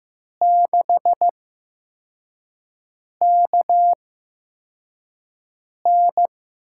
Morse code, heard as a keyed tone, about 15 wpm, spaced as for 4 wpm, 700 Hz. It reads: 6KN
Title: USA Call Signs - 6 Characters - 8x Character Spacing - 15wpm